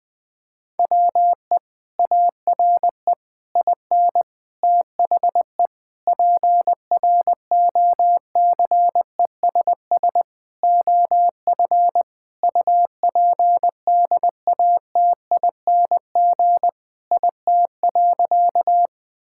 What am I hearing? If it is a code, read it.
Morse code, heard as a keyed tone, about 20 wpm, 700 Hz. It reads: WE ARE IN THE PROCESS OF UPDATING IT.
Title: Sentences with 6-Word Phrases - 20wpm